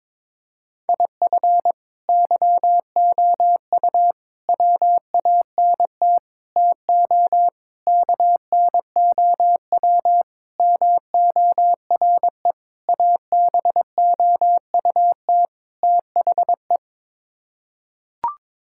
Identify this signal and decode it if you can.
Morse code, heard as a keyed tone, about 22 wpm, 700 Hz. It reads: IF YOU WANT TO KNOW MORE ABOUT THE
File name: practice / 22wpm / Sets of 8 Words - 22wpm